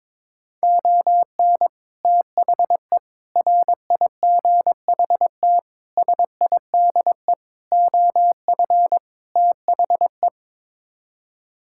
Morse code, heard as a keyed tone, about 22 wpm, 700 Hz. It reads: ON THE RIGHT SIDE OF THE
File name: practice / 22wpm / Sets of 6 Words - 22wpm